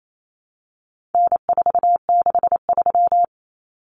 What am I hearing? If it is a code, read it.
Morse code, heard as a keyed tone, about 28 wpm, 700 Hz. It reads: N463
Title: Common Cut Numbers - 4-Digit 28wpm